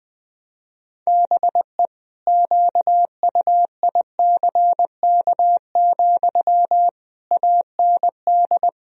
Morse code, heard as a keyed tone, about 20 wpm, 700 Hz. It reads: BE QUICK, AND
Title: Sentences from Top 1000 Words - Review Part 3 - 20wpm